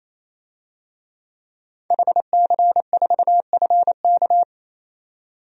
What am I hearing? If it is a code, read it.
Morse code, heard as a keyed tone, about 28 wpm, 700 Hz. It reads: HC4FK
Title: DXCC-100 Call Signs - 5 Characters - 28wpm